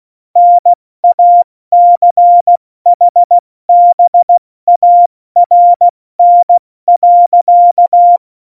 Morse code, heard as a keyed tone, about 16 wpm, 700 Hz. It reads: NACHBARN.
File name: Morsekurs / Witze / Witz 21 Juli 2020, 70 F80